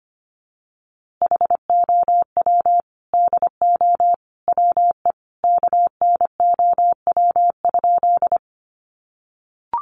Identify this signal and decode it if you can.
Morse code, heard as a keyed tone, about 25 wpm, 700 Hz. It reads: HOW DO WE KNOW?
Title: A Sound Of Thunder 25wpm